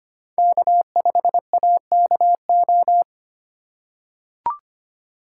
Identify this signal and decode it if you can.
Morse code, heard as a keyed tone, about 25 wpm, 700 Hz. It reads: K5AKO
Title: USA Call Signs - 5 Characters - 4x Character Spacing - 25wpm